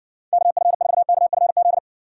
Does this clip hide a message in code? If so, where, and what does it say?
Yes: Morse code from 0.3 to 1.8 s, heard as a keyed tone, about 60 wpm, 700 Hz. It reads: KF4KFZ